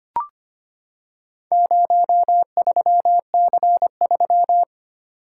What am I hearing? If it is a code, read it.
Morse code, heard as a keyed tone, about 26 wpm, 700 Hz. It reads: 03C3
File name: practice / 25wpm / Four Letter-Number Combinations - 25wpm